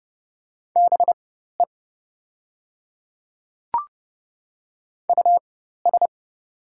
Morse code, heard as a keyed tone, about 30 wpm, spaced as for 14 wpm, 690 Hz. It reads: BE US
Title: 4x Character Spacing - Top 200 Words - 30wpm